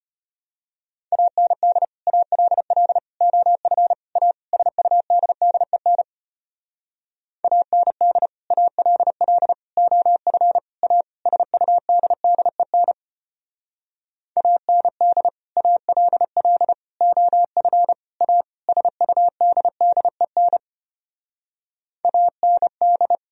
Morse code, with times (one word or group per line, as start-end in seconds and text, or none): 1.1-1.8 AND
2.1-3.0 ALL
3.2-3.9 OF
4.2-4.3 A
4.5-6.0 SUDDEN
7.4-8.3 AND
8.5-9.5 ALL
9.8-10.6 OF
10.8-11.0 A
11.3-12.9 SUDDEN
14.4-15.3 AND
15.6-16.7 ALL
17.0-17.9 OF
18.2-18.4 A
18.7-20.6 SUDDEN
22.1-23.2 AND